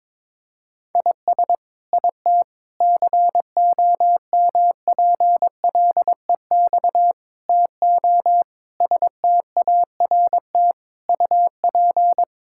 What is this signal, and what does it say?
Morse code, heard as a keyed tone, about 21 wpm, 700 Hz. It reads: IS IT COMPLEX TO START UP
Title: Sentences from Top 1000 Words - Review Part 6 - 22wpm